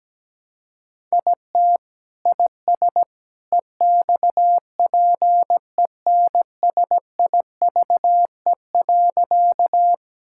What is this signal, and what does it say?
Morse code, heard as a keyed tone, about 17 wpm, 700 Hz. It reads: IT IS EXPENSIVE.
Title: Sentences from Top 1000 Words - Review Part 6 - 17wpm